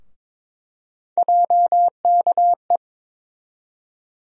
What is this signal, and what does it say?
Morse code, heard as a keyed tone, about 21 wpm, 700 Hz. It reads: JKE
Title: USA Call Sign Postfixes - 3 Letters 22wpm